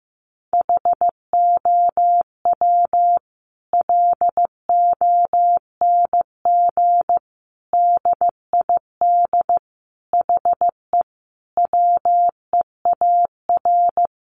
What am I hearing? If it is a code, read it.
Morse code, heard as a keyed tone, about 15 wpm, 700 Hz. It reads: HOW LONG DID HE WEAR
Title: Sentences from Top 1500 Words 15wpm